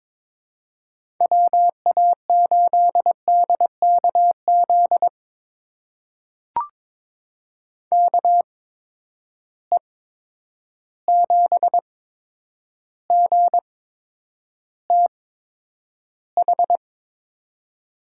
Morse code, heard as a keyed tone, about 22 wpm, spaced as for 9 wpm, 700 Hz. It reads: WA8DKZ KE7GTH
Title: USA Call Signs - 6 Characters - 8x Character Spacing - 22wpm